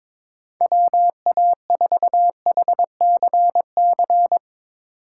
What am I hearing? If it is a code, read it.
Morse code, heard as a keyed tone, about 21 wpm, 700 Hz. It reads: WA4HCC
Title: K1USN SST QSO Practice - 22wpm